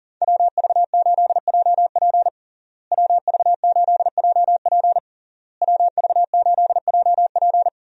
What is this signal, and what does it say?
Morse code, heard as a keyed tone, about 40 wpm, 700 Hz. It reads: WV8JP WV8JP WV8JP
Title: USA Call Signs - 5 Characters - Repeated 3x - 40wpm